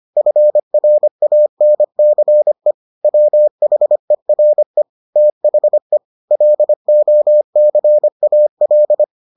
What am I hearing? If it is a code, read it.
Morse code, heard as a keyed tone, about 25 wpm, 600 Hz. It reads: FRANCE WHERE THE LOCAL